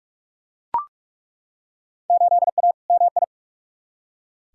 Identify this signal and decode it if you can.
Morse code, heard as a keyed tone, about 41 wpm, 700 Hz. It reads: 8A MI
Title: Field Day Exchanges Encore - 45wpm